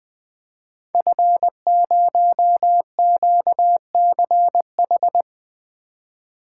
Morse code, heard as a keyed tone, about 20 wpm, 700 Hz. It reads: F0QCH
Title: Code Groups - 5 Letter:Numbers - 20wpm